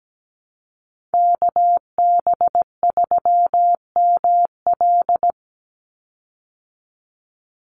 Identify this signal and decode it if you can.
Morse code, heard as a keyed tone, about 17 wpm, 700 Hz. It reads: KB3ML
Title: Call Signs Repeated 3x - 17wpm